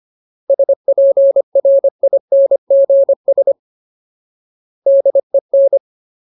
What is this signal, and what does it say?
Morse code, heard as a keyed tone, about 25 wpm, 550 Hz. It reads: SPRINGS DEN
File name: Ultimate Word List / 25WPM Ultimate